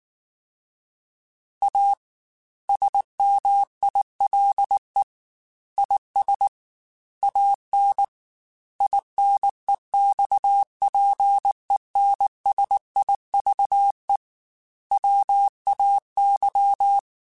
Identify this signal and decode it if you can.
Morse code, heard as a keyed tone, about 19 wpm, 800 Hz. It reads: A SMILE IS AN INEXPENSIVE WAY
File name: Quotes / Quotes-mp3s / A Smile @18